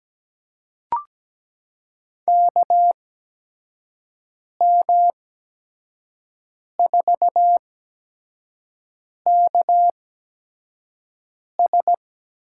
Morse code, heard as a keyed tone, about 17 wpm, spaced as for 5 wpm, 700 Hz. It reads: KM4KS
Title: USA Call Signs - 6 Characters - 8x Character Spacing - 17wpm